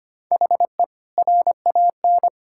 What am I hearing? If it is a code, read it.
Morse code, heard as a keyed tone, about 25 wpm, 700 Hz. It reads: HE RAN
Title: Sentences from Top 1000 Words - Review Part 6 - 25wpm